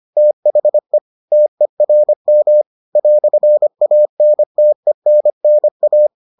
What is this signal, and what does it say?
Morse code, heard as a keyed tone, about 25 wpm, 600 Hz. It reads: THE TERM "ANTENNA